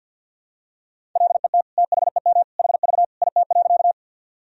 Morse code, heard as a keyed tone, about 50 wpm, 700 Hz. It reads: LET THEM HV IT.
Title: Sentences from Top 300 Words and Abbreviations 50wpm